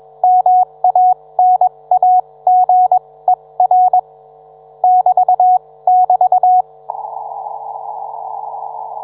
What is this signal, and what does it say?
Morse code, heard as a keyed tone, about 21 wpm, spaced as for 18 wpm, 730 Hz. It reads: MANAGER <BT><BT>
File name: book